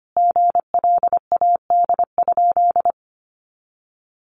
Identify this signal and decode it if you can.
Morse code, heard as a keyed tone, about 25 wpm, 700 Hz. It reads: GLAD?
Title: Sentences from Top 800 Words 25wpm